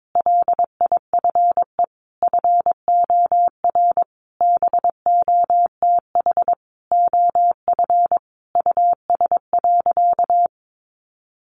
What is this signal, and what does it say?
Morse code, heard as a keyed tone, about 22 wpm, 700 Hz. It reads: LIFE FOR BOTH OF US.
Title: Sentences from Top 1100 Words 22wpm